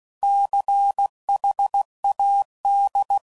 Morse code, heard as a keyed tone, about 16 wpm, 800 Hz. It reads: CHAD
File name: Prefixes / By First Letter / Cs